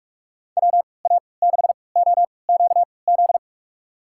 Morse code, heard as a keyed tone, about 46 wpm, spaced as for 26 wpm, 700 Hz. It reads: WA6OQZ
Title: USA Call Signs - 6 Characters - 3x Character Spacing - 45wpm